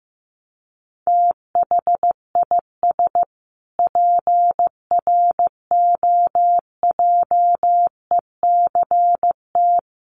Morse code, heard as a keyed tone, about 15 wpm, 700 Hz. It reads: THIS PROJECT
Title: Sentences from Top 1400 Words 15wpm